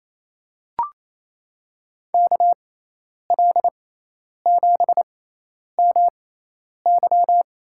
Morse code, heard as a keyed tone, about 28 wpm, spaced as for 10 wpm, 700 Hz. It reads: KL7MY
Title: USA Call Signs - 6x Character Spacing - 28wpm